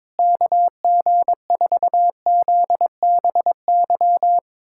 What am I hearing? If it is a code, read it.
Morse code, heard as a keyed tone, about 22 wpm, 700 Hz. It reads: KG4ZBY
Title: USA Callsign DE Callsign 22wpm